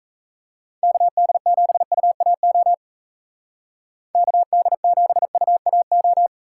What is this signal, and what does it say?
Morse code, heard as a keyed tone, about 34 wpm, 700 Hz. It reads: KD7UAO KD7UAO
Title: Speed Racing - USA 2x3 Call Signs 28wpm